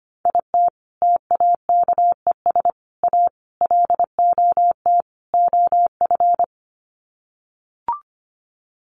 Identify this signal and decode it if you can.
Morse code, heard as a keyed tone, about 25 wpm, 700 Hz. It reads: IT TAKES A LOT OF
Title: Sets of 5 Words Encore 25wpm